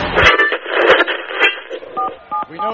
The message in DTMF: 477